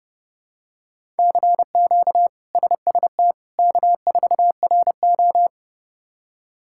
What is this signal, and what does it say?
Morse code, heard as a keyed tone, about 30 wpm, 700 Hz. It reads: CQ SST K4RO
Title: K1USN SST QSO Practice - 30wpm